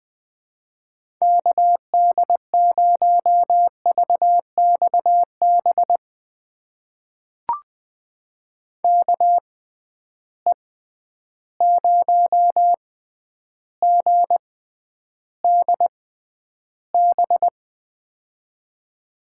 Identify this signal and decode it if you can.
Morse code, heard as a keyed tone, about 20 wpm, spaced as for 10 wpm, 700 Hz. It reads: KD0VXB KE0GDB
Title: USA Call Signs - 6 Characters - 6x Character Spacing - 20wpm